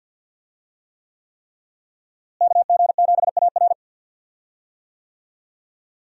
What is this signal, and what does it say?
Morse code, heard as a keyed tone, about 52 wpm, 700 Hz. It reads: KG7AR
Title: USA Call Signs - 5 Characters - Repeated 3x - 50wpm